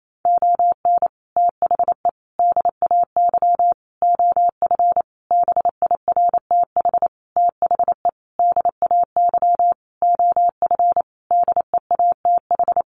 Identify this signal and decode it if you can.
Morse code, heard as a keyed tone, about 28 wpm, 700 Hz. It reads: ON THE DAY OF BIRTH THE DAY OF DEATH
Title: Sentences from Top 1100 Words 28wpm